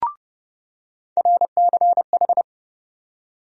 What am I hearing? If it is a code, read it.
Morse code, heard as a keyed tone, about 31 wpm, 700 Hz. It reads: RCH
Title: USA Call Sign Postfixes - 3 Letters 30wpm